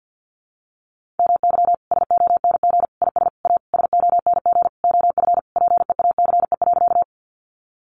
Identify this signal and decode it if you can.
Morse code, heard as a keyed tone, about 50 wpm, 700 Hz. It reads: MY SONG IS A SONG OF PEACE.
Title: Sentences from Top 1500 Words 50wpm